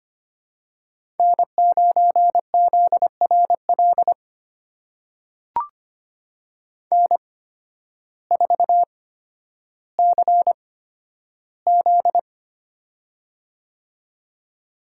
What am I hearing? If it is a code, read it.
Morse code, heard as a keyed tone, about 25 wpm, spaced as for 12 wpm, 700 Hz. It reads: N9ZRL N4CZ
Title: USA Call Signs - 8x Character Spacing - 25wpm